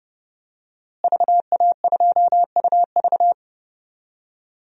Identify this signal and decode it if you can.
Morse code, heard as a keyed tone, about 30 wpm, 700 Hz. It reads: VA2UV
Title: DXCC-100 Call Signs - 5 Characters - Repeated 2x - 30wpm